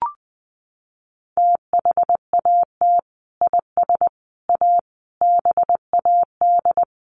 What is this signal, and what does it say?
Morse code, heard as a keyed tone, about 20 wpm, 700 Hz. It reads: THAT IS A BAD
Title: Sentences from Top 700 Words 20wpm